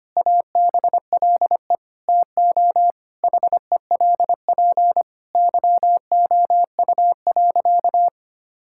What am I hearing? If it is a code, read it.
Morse code, heard as a keyed tone, about 25 wpm, 700 Hz. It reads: ABLE TO HELP YOU.
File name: practice / 25wpm / Sentences with 4-Word Phrases - 25wpm